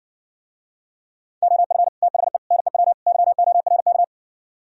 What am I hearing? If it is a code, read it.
Morse code, heard as a keyed tone, about 58 wpm, 700 Hz. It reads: OF THE NEW YORK